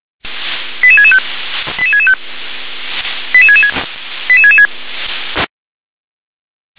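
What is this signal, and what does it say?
Five-tone selective call (ZVEI), heard as quick runs of five tones, tones of about 70 ms. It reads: 88665 88665 88666 88666
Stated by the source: ZVEI five-tone selective call 88641 received on SDR.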